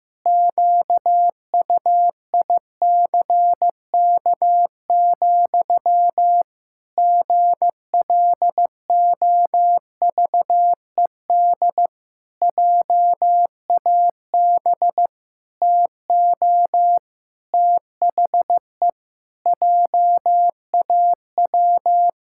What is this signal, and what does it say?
Morse code, heard as a keyed tone, about 15 wpm, 700 Hz. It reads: QUICK, GLOVED JAB TO THE JAW
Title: Pangram Challenge v2 - 15wpm